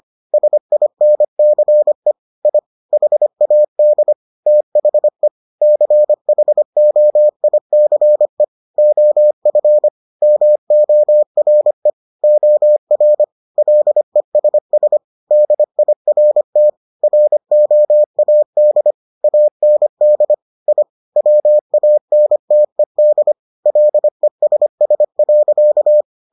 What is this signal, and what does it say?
Morse code, heard as a keyed tone, about 25 wpm, 600 Hz. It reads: SINCE I HAD THE CHOICE OF MORE OR LESS DIRT ROAD AND I WANTED LESS.